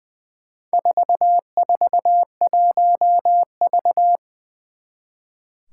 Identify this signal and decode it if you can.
Morse code, heard as a keyed tone, about 20 wpm, 700 Hz. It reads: 441V
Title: Four Challenging Letter-Number Combinations 20wpm